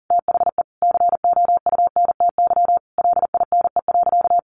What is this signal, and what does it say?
Morse code, heard as a keyed tone, about 40 wpm, 700 Hz. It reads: THE COUNTY LINE.
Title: Sentences from Top 1200 Words 40wpm